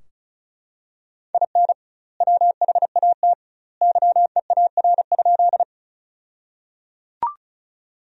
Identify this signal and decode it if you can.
Morse code, heard as a keyed tone, about 34 wpm, 700 Hz. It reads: IN WHAT YEAR?
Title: 2x Word-Spacing - Sentences from Top 200 Words - 35wpm